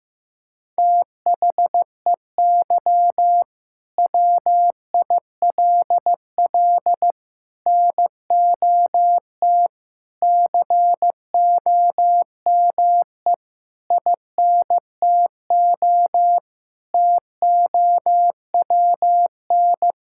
Morse code, heard as a keyed tone, about 15 wpm, 700 Hz. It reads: THEY WILL NOT COME INTO TOWN
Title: Sentences from Top 1000 Words - Review Part 5 - 15wpm